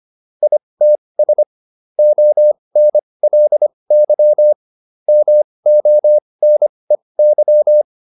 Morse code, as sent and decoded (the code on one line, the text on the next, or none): .. - ... / --- -. .-.. -.-- / -- --- -. -.--
ITS ONLY MONEY